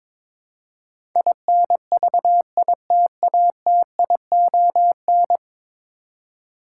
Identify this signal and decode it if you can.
Morse code, heard as a keyed tone, about 22 wpm, 700 Hz. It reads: INVITATION
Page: morse-code-ninja-practice-22wpm